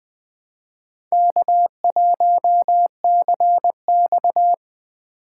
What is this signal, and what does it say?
Morse code, heard as a keyed tone, about 20 wpm, 700 Hz. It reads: K1CX
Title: K1USN SST QSO Practice - 20wpm